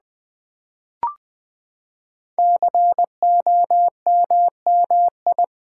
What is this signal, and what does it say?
Morse code, heard as a keyed tone, about 20 wpm, 700 Hz. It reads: COMMI
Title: Build A Word - 5 Letters At A Time - 12-Letter Words - 20wpm